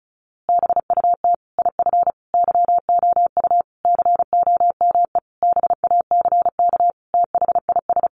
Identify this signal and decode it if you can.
Morse code, heard as a keyed tone, about 34 wpm, 700 Hz. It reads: BUT IF YOU COME BACK THIS
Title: A Sound Of Thunder 35wpm